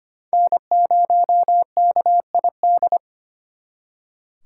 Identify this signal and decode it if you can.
Morse code, heard as a keyed tone, about 33 wpm, 700 Hz. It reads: N0KID